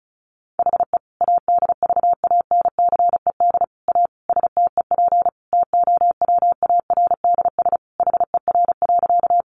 Morse code, heard as a keyed tone, about 34 wpm, 700 Hz. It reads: HE ADVANCED A STEP TOWARDS HER.
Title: Sentences from Top 1200 Words 35wpm